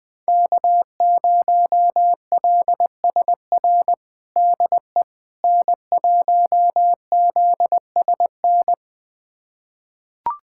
Morse code, heard as a keyed tone, about 20 wpm, 700 Hz. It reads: K0LSR DE N1ZSN
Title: USA Callsign DE Callsign - 5 Characters - 20wpm